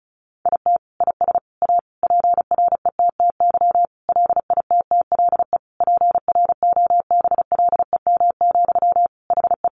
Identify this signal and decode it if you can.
Morse code, heard as a keyed tone, about 34 wpm, 700 Hz. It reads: IT IS A PRETTY LITTLE PROBLEM, HE
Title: Sentences from Top 1000 Words 35wpm